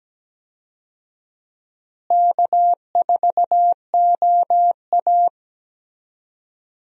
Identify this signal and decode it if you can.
Morse code, heard as a keyed tone, about 17 wpm, 700 Hz. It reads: K4OA
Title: USA Call Signs - 4 Characters - Repeated 2x - 17wpm